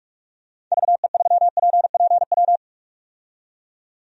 Morse code, heard as a keyed tone, about 46 wpm, 700 Hz. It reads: VE3PPW
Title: DXCC-100 Call Signs - 6 Characters - 45wpm